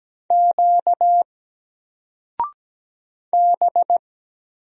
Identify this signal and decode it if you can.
Morse code, heard as a keyed tone, about 17 wpm, 700 Hz. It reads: Q B
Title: Single Character (Letter:Number:Punctuation:Prosign) Alt 17wpm